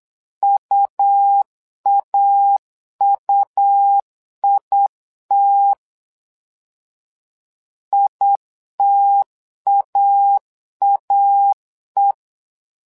Morse code, heard as a keyed tone, about 8 wpm, 800 Hz. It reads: UAUIT ITAAE
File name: LEZIONI / 2 UHDN / 2 U 8wpm 2028